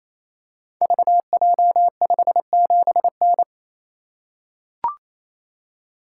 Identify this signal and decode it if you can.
Morse code, heard as a keyed tone, about 27 wpm, 700 Hz. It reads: VJ57N
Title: Code Groups - 5 Letter:Numbers - 28wpm